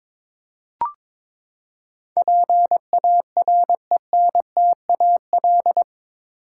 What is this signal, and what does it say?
Morse code, heard as a keyed tone, about 22 wpm, 700 Hz. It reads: PARENTAL
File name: practice / 22wpm / Eight Letter Words - 22wpm